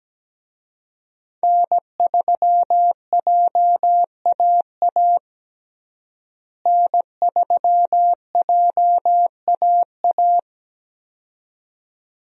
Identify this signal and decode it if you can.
Morse code, heard as a keyed tone, about 17 wpm, 700 Hz. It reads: N3JAA N3JAA
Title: USA Call Signs - 5 Characters - Repeated 2x - 17wpm